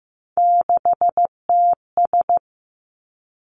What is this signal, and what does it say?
Morse code, heard as a keyed tone, about 15 wpm, 700 Hz. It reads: 6TS
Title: Three Challenging Letter-Number Combinations 15wpm